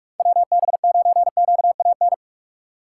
Morse code, heard as a keyed tone, about 56 wpm, 700 Hz. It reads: WB9QAN
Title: Speed Racing - USA 2x3 Call Signs 45wpm